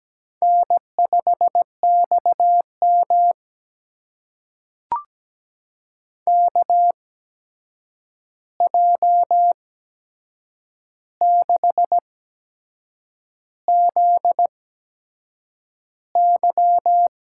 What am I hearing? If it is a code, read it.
Morse code, heard as a keyed tone, about 17 wpm, spaced as for 7 wpm, 700 Hz. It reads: N5XM KJ6ZY
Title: USA Call Signs - 8x Character Spacing - 17wpm